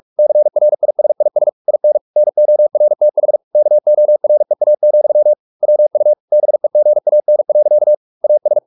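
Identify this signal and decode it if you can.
Morse code, heard as a keyed tone, about 44 wpm, 600 Hz. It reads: CRISIS IN NORTH KOREA, WU BEGAN. AS
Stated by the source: Morse practice at 45 wpm, the novel Never by Ken Follett in English, from CW med Gnister.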